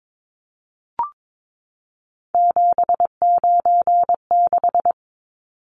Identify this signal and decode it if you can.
Morse code, heard as a keyed tone, about 22 wpm, 700 Hz. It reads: 796
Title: Numbers - 3-Digit 22wpm